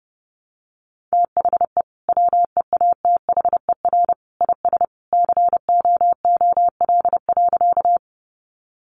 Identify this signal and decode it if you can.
Morse code, heard as a keyed tone, about 30 wpm, 700 Hz. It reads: THE WEATHER IS COOL.
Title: Sentences from Top 1200 Words 30wpm